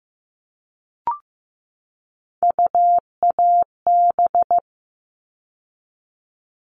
Morse code, heard as a keyed tone, about 15 wpm, 700 Hz. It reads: UAB
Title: All Cut Numbers - 3-Digit 15wpm